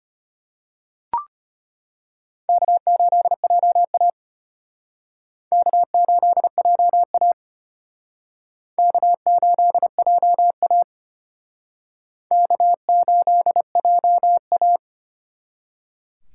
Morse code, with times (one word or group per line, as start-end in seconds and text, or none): 2.5-4.1 K8JA
5.5-7.3 K8JA
8.8-10.8 K8JA
12.3-14.8 K8JA